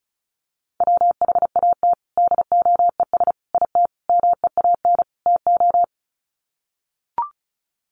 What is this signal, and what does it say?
Morse code, heard as a keyed tone, about 34 wpm, 700 Hz. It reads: WHAT DOES IT MEAN TO
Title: Sets of 5 Words Encore2 35wpm